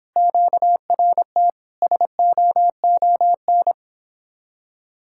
Morse code, heard as a keyed tone, about 26 wpm, 700 Hz. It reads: QRT SOON